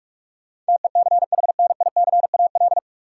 Morse code, heard as a keyed tone, about 46 wpm, 700 Hz. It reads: TECHNICAL
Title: Sentences from Top 1100 Words - New Word Drill Alt - 45wpm